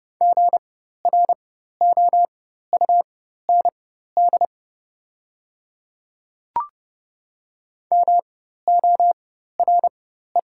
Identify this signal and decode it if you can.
Morse code, heard as a keyed tone, about 30 wpm, spaced as for 14 wpm, 700 Hz. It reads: GROUND MORE
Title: 4x Character Spacing - Top 500 Words - 30wpm